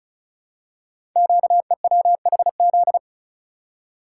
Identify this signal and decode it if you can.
Morse code, heard as a keyed tone, about 36 wpm, 700 Hz. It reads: QEWHZ
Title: Code Groups - 5 Letter:Numbers - 35wpm